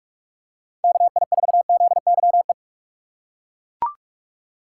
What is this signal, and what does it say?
Morse code, heard as a keyed tone, about 41 wpm, 700 Hz. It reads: KI4ZYE
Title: USA Call Signs - 6 Characters - 4x Character Spacing - 45wpm